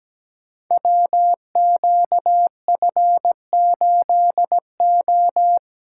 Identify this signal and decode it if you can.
Morse code, heard as a keyed tone, about 17 wpm, 700 Hz. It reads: WQF8O
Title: Code Groups - 5 Letter:Numbers - 17wpm